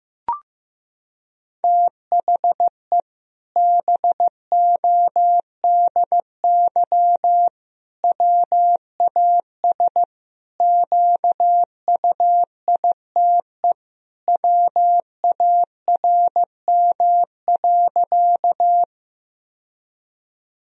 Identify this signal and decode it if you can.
Morse code, heard as a keyed tone, about 15 wpm, 700 Hz. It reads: THE BODY WAS QUITE WARM.
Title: Sentences from Top 1000 Words - Review Part 2 - 15wpm